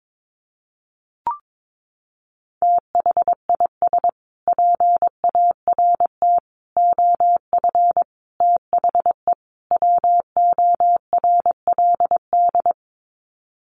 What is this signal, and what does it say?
Morse code, heard as a keyed tone, about 22 wpm, 700 Hz. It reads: THIS PART OF THE WORLD
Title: Sets of 5 Words Encore2 22wpm